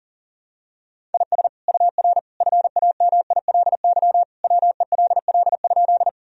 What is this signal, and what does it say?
Morse code, heard as a keyed tone, about 40 wpm, 700 Hz. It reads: IS UR FAMILY WELL?
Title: Sentences from Top 400 Words and Abbreviations 40wpm